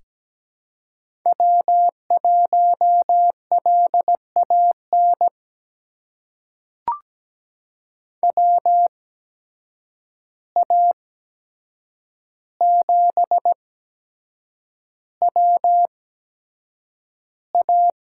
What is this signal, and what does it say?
Morse code, heard as a keyed tone, about 17 wpm, spaced as for 7 wpm, 700 Hz. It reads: W1LAN WA7WA
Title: USA Call Signs - 5 Characters - 8x Character Spacing - 17wpm